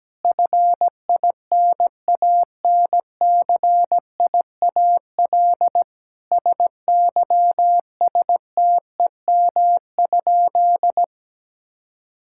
Morse code, heard as a keyed tone, about 17 wpm, 700 Hz. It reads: FINANCIAL SYSTEM?
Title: Sentences from Top 1000 Words - Review Part 5 - 17wpm